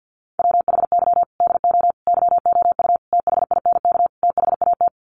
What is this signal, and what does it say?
Morse code, heard as a keyed tone, about 48 wpm, 700 Hz. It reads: WHY DO YOU THINK THAT
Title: Sets of 5 Words Encore2 50wpm